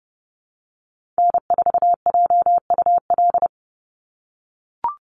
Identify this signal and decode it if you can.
Morse code, heard as a keyed tone, about 30 wpm, 700 Hz. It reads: N4JUL
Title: USA Call Signs 30wpm